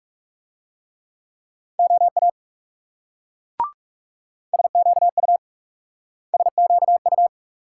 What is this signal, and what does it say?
Morse code, heard as a keyed tone, about 38 wpm, 700 Hz. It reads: OA SQU SQU